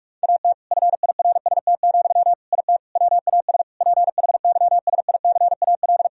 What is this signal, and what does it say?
Morse code, heard as a keyed tone, about 46 wpm, 700 Hz. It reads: AT FIRST, IT WAS PHYSICAL
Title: Sentences from Top 1000 Words - Review Part 4 - 45wpm